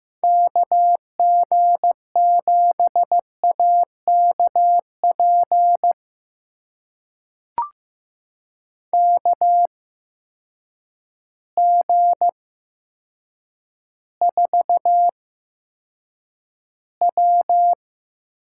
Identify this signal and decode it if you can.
Morse code, heard as a keyed tone, about 15 wpm, spaced as for 7 wpm, 700 Hz. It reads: KG7AKP KG4W
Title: USA Call Signs - 6 Characters - 8x Character Spacing - 15wpm